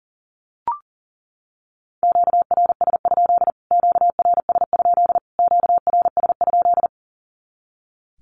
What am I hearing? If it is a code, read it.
Morse code, heard as a keyed tone, about 40 wpm, 700 Hz. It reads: QRS? QRS? QRS?
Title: Q-Codes Repeated 3x 40wpm